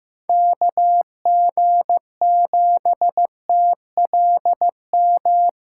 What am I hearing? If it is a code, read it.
Morse code, heard as a keyed tone, about 15 wpm, 700 Hz. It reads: KG7TLM
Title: USA Callsign DE Callsign - 6 Characters - 15wpm